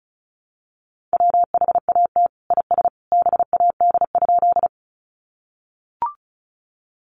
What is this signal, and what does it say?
Morse code, heard as a keyed tone, about 34 wpm, 700 Hz. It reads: WHAT IS BAD?
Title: Sentences from Top 700 Words 35wpm